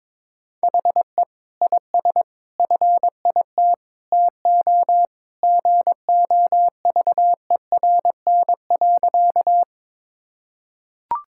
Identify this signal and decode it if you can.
Morse code, heard as a keyed tone, about 22 wpm, 700 Hz. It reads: HE IS FIT TO GOVERN.
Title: Sentences from Top 1000 Words - Review Part 5 - 22wpm